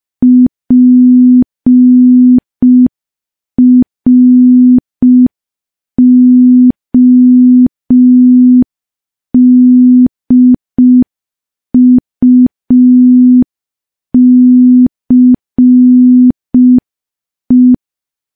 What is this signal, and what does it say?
Morse code, heard as a keyed tone, about 5 wpm, 250 Hz. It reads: PRODUCE
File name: morse / morse-training-05WPM